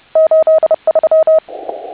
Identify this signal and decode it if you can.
Morse code, heard as a keyed tone, about 30 wpm, 1280 Hz, with noise. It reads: 83